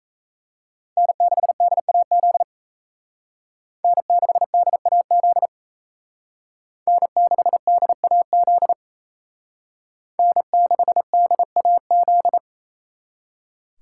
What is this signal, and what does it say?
Morse code, heard as a keyed tone, about 35 wpm, 700 Hz. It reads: N6DAZ N6DAZ N6DAZ N6DAZ